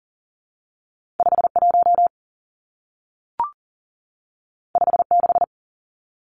Morse code, heard as a keyed tone, about 42 wpm, 700 Hz. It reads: 5 1 5 6
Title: Two Challenging Letter-Number Combinations 40wpm